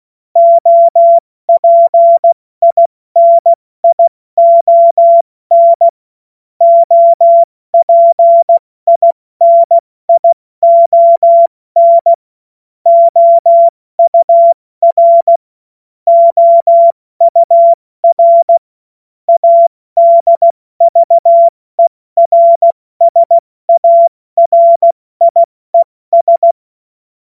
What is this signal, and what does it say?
Morse code, heard as a keyed tone, about 16 wpm, 680 Hz. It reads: OPINION OPINION OUR OUR ADVERSARIES
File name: Morsekurs / Witze / Witz 23 Januar 2024 70 F80